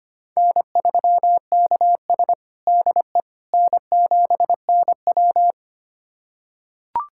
Morse code, heard as a keyed tone, about 25 wpm, 700 Hz. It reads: N3KS DE N7NW
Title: USA Callsign DE Callsign - 4 Characters - 25wpm